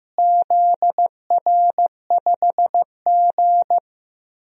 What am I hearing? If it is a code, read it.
Morse code, heard as a keyed tone, about 15 wpm, 700 Hz. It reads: ZR5G
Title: DXCC-100 Call Signs - 4 Characters - Repeated 3x - 15wpm